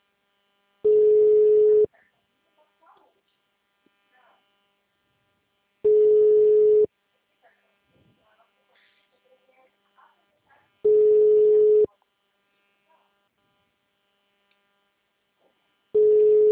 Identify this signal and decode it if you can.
Ringback tone (Europe, 425 Hz), with speech.